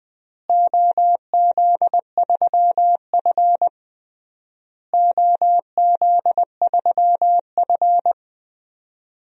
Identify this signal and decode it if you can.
Morse code, heard as a keyed tone, about 20 wpm, 700 Hz. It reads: OZ3F OZ3F
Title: DXCC-100 Call Signs - 4 Characters - Repeated 3x - 20wpm